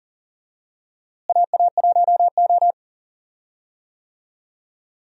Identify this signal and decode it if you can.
Morse code, heard as a keyed tone, about 40 wpm, 700 Hz. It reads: AA1O
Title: USA Call Signs - 4 Characters - Repeated 2x - 40wpm